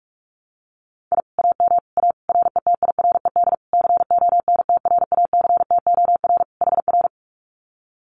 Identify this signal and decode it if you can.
Morse code, heard as a keyed tone, about 44 wpm, 700 Hz. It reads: I AM A RETIRED CONTRACTOR HR
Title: Common QSO Phrases Encore 45wpm